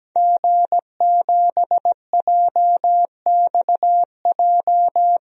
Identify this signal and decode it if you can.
Morse code, heard as a keyed tone, about 17 wpm, 700 Hz. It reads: G7JXJ
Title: DXCC-100 Call Signs - 5 Characters - 17wpm